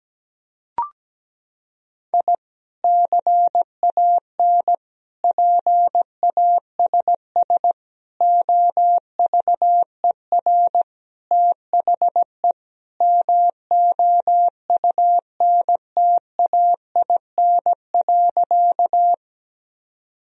Morse code, heard as a keyed tone, about 17 wpm, 700 Hz. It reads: I CAN PASS OVER THE MOUNTAIN.
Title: Sentences from Top 1000 Words - Review Part 1 - 17wpm